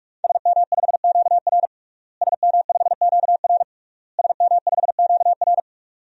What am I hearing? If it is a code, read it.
Morse code, heard as a keyed tone, about 46 wpm, 700 Hz. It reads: SM5QR SM5QR SM5QR
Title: DXCC-100 Call Signs - 5 Characters - Repeated 3x - 45wpm